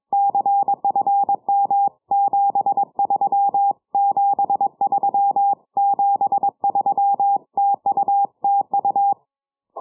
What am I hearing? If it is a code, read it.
Morse code, heard as a keyed tone, about 22 wpm, 800 Hz. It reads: CFM 73 73 73 TU TU
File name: iw0ffk / 70cm / 2007 10 07 UR7D IW0FFK 6 980km